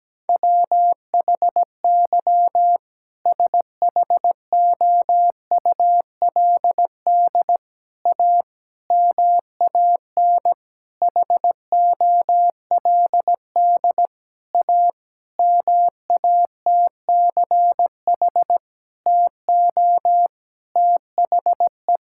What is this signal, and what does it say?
Morse code, heard as a keyed tone, about 17 wpm, 700 Hz. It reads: WHY SHOULD A MAN HOLD A MATCH TO THE
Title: Sentences from Top 1600 Words 17wpm